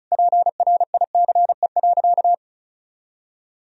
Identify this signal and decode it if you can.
Morse code, heard as a keyed tone, about 36 wpm, 700 Hz. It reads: PRICE.
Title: Sentences from Top 1000 Words - Review Part 4 - 35wpm